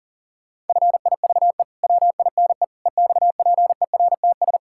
Morse code, heard as a keyed tone, about 40 wpm, 700 Hz. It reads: FIVE WINE EXPERTS
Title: Pangram Challenge v2 - 40wpm